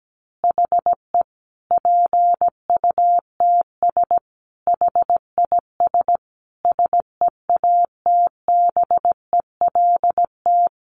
Morse code, heard as a keyed tone, about 17 wpm, 700 Hz. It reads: HE PUTS HIS SEATBELT
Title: Additional Easy Phrases 17wpm